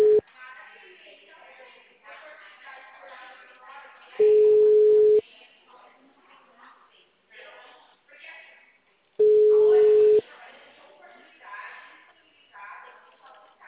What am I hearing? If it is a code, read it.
Ringback tone (Europe, 425 Hz), with speech.
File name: out- -2044- - - .18628